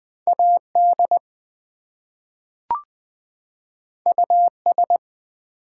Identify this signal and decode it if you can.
Morse code, heard as a keyed tone, about 20 wpm, 700 Hz. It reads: AD US
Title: Two Letter Words - 20wpm